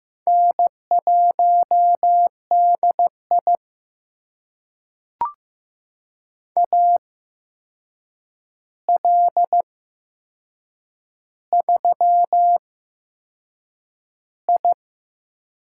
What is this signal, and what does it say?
Morse code, heard as a keyed tone, about 15 wpm, spaced as for 6 wpm, 700 Hz. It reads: N1DI AL3I
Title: USA Call Signs - 4 Characters - 8x Character Spacing - 15wpm